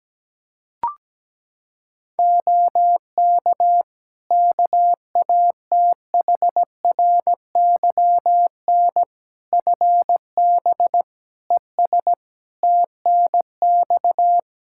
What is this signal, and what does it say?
Morse code, heard as a keyed tone, about 17 wpm, 700 Hz. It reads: OK KATHRYN FB ES TNX
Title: K1USN SST QSO Practice 17wpm